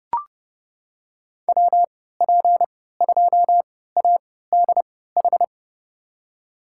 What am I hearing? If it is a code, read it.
Morse code, heard as a keyed tone, about 30 wpm, spaced as for 17 wpm, 700 Hz. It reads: WP2ADH
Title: USA Call Signs - 3x Character Spacing - 30wpm